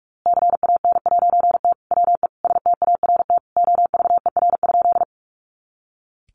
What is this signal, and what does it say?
Morse code, heard as a keyed tone, about 45 wpm, 700 Hz. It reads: CAN'T WE START OVER?
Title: A Sound Of Thunder 45wpm